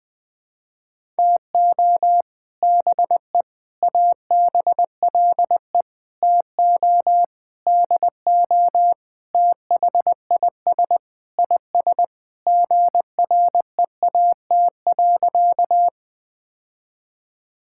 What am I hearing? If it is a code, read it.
Morse code, heard as a keyed tone, about 21 wpm, 700 Hz. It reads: TO BE ABLE TO DO THIS IS GREAT.
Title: Sentences with 5-Word Phrases - 20wpm